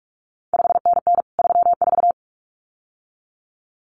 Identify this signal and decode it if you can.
Morse code, heard as a keyed tone, about 46 wpm, 700 Hz. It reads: 5NN 34
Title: CQ WW DX Contest Exchange Encore - 45wpm